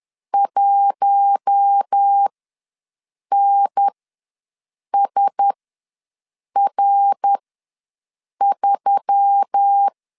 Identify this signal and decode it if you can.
Morse code, heard as a keyed tone, about 10 wpm, spaced as for 6 wpm, 800 Hz. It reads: 1NSR3
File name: LEZIONI / 5 LBVC / 5 LB ws 10wpm 3616